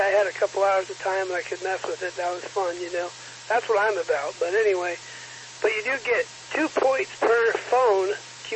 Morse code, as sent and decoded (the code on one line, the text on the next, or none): none